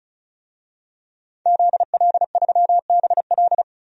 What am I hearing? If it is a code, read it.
Morse code, heard as a keyed tone, about 36 wpm, 700 Hz. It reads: ZL3BL
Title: DXCC-100 Call Signs - 5 Characters - 35wpm